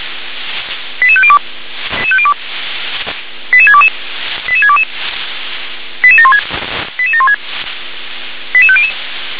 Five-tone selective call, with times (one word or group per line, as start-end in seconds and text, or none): 1.0-1.4 88502
3.5-3.9 88622
4.5-4.8 88622
6.0-6.4 88716
7.0-7.3 88716
8.5-8.9 88500